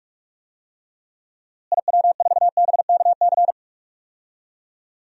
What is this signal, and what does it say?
Morse code, heard as a keyed tone, about 44 wpm, 700 Hz. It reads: IW4BKC
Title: DXCC-100 Call Signs - 6 Characters - Repeated 2x - 45wpm